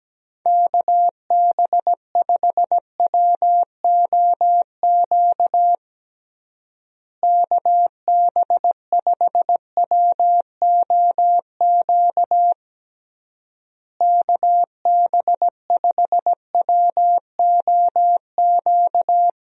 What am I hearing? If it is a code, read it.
Morse code, heard as a keyed tone, about 17 wpm, 700 Hz. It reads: KB5WOQ KB5WOQ KB5WOQ
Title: USA Call Signs - 6 Characters - Repeated 3x - 17wpm